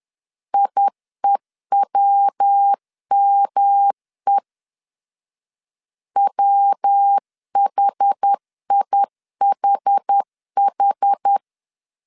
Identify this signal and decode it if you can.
Morse code, heard as a keyed tone, about 10 wpm, 800 Hz. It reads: IEWME WHIHH
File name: 10wpm